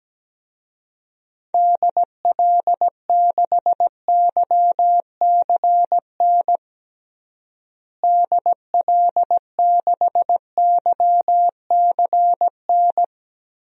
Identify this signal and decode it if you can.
Morse code, heard as a keyed tone, about 17 wpm, 700 Hz. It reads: DL6YCN DL6YCN
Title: DXCC-100 Call Signs - 6 Characters - Repeated 3x - 17wpm